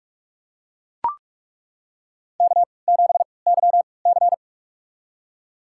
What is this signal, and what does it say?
Morse code, heard as a keyed tone, about 41 wpm, spaced as for 25 wpm, 700 Hz. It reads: K7YC
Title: USA Call Signs - 3x Character Spacing - 45wpm